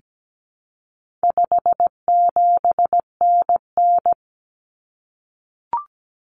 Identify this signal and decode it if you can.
Morse code, heard as a keyed tone, about 17 wpm, 700 Hz. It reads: 57NN
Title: Common Cut Numbers - 4-Digit 17wpm